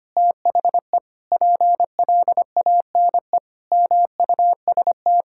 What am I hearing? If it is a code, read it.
Morse code, heard as a keyed tone, about 25 wpm, 700 Hz. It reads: THE PLANE MUST
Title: Sentences from Top 1000 Words - Review Part 6 - 25wpm